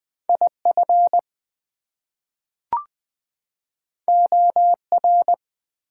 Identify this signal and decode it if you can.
Morse code, heard as a keyed tone, about 20 wpm, 700 Hz. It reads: IF OR
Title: Two Letter Words - 20wpm